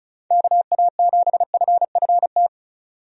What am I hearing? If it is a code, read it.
Morse code, heard as a keyed tone, about 36 wpm, 700 Hz. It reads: KA7FFT
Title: USA Call Signs - 6 Characters - 35wpm